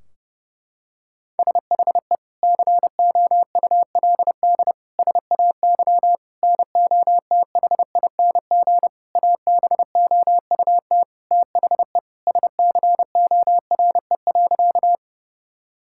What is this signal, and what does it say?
Morse code, heard as a keyed tone, about 32 wpm, 700 Hz. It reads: SHE COULD SAY NOTHING ABOUT THE SCORE.
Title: Sentences from Top 1600 Words 30wpm